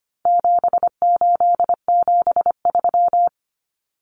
Morse code, heard as a keyed tone, about 25 wpm, 700 Hz. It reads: 7873
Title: Common Cut Numbers - 4-Digit 25wpm